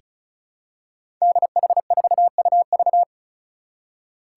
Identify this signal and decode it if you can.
Morse code, heard as a keyed tone, about 35 wpm, 700 Hz. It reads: DH4UV
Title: Five Challenging Letter-Number Combinations 35wpm